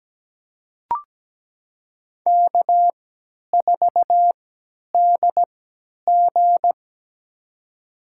Morse code, heard as a keyed tone, about 17 wpm, spaced as for 10 wpm, 700 Hz. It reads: K4DG
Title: USA Call Signs - 3x Character Spacing - 17wpm